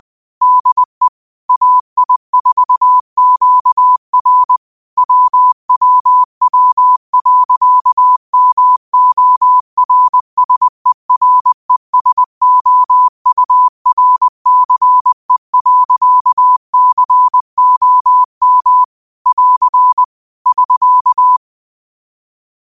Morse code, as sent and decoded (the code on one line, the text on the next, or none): -.. / .- .. ....- --.- .-. / .-- .-- .-- .-.-.- -- --- .-. ... .-. ... --- ..- .-. -.-. .-.-.- -.-. --- -- / .-.-. / ...-.-
DE AI4QR WWW.MORSERESOURCE.COM <AR> <SK>